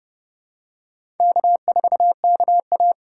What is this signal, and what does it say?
Morse code, heard as a keyed tone, about 30 wpm, 700 Hz. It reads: K4KA
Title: USA Call Signs - 4 Characters - 4x Character Spacing - 30wpm